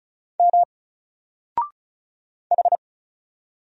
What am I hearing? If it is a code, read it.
Morse code, heard as a keyed tone, about 34 wpm, 700 Hz. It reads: M H
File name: Single-Letter-Number-35wpm